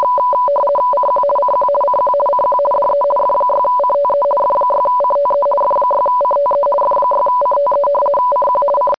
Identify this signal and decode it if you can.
RTTY, about 50 baud. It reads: CQ CQ CQ CQ DE F6CTE F6CTE F6CTE F6CTE CQ C
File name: rtty50